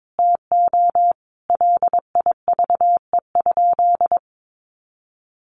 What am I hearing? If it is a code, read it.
Morse code, heard as a keyed tone, about 22 wpm, 700 Hz. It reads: TO LIVE?
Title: Sentences from Top 1500 Words 22wpm